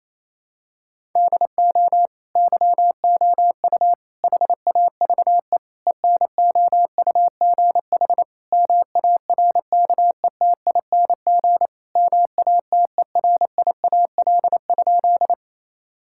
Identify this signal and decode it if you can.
Morse code, heard as a keyed tone, about 28 wpm, 700 Hz. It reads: DO YOU HAVE ENOUGH MARKETING MATERIAL?
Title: Sentences from Top 1000 Words - Review Part 5 - 28wpm